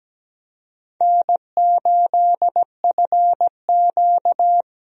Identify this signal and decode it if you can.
Morse code, heard as a keyed tone, about 17 wpm, 700 Hz. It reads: N8FQ
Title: USA Call Signs - 4 Characters - Repeated 3x - 17wpm